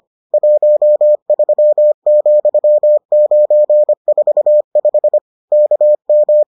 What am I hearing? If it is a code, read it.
Morse code, heard as a keyed tone, about 25 wpm, 600 Hz. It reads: 13,945 KM